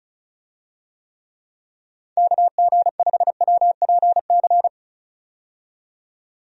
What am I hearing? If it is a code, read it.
Morse code, heard as a keyed tone, about 34 wpm, 700 Hz. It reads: KG5WPC
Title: USA Call Signs - 6 Characters - 35wpm